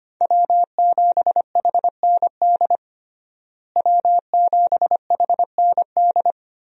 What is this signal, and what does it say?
Morse code, heard as a keyed tone, about 25 wpm, 700 Hz. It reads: W7HND W7HND
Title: USA Call Signs - 5 Characters - Repeated 2x - 25wpm